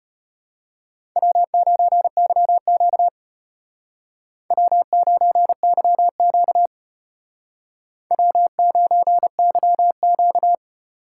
Morse code, heard as a keyed tone, about 32 wpm, 700 Hz. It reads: W9YQ W9YQ W9YQ